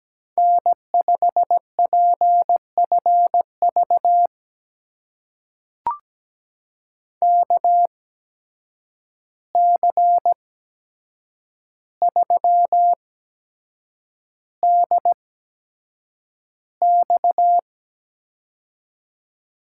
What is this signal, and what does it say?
Morse code, heard as a keyed tone, about 17 wpm, spaced as for 7 wpm, 700 Hz. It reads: N5PFV KC3DX
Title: USA Call Signs - 5 Characters - 8x Character Spacing - 17wpm